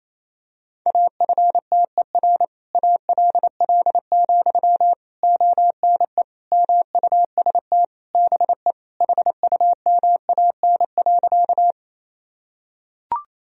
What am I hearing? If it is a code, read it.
Morse code, heard as a keyed tone, about 28 wpm, 700 Hz. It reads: AFTER ALL, ONE MUST BE HUMAN.
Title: Sentences from Top 1000 Words - Review Part 3 - 28wpm